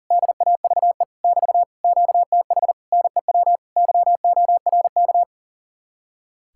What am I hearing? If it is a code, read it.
Morse code, heard as a keyed tone, about 48 wpm, 700 Hz. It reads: DAVE <BT> QTH NEW YORK
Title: Speed-Racing Common QSO Phrases Encore - 40wpm